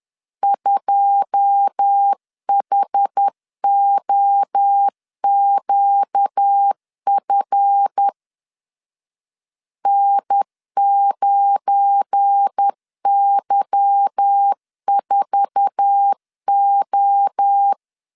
Morse code, heard as a keyed tone, about 10 wpm, 800 Hz. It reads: 2HOQF N9Y4O
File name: CW / DET / 1 9 1500C S A 10wpm 4511